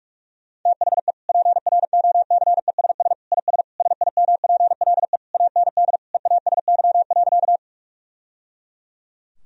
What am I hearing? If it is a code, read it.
Morse code, heard as a keyed tone, about 41 wpm, 700 Hz. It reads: THE PROCESS IS SIMPLE AND EASY.
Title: Sentences from Top 1000 Words - Review Part 3 - 45wpm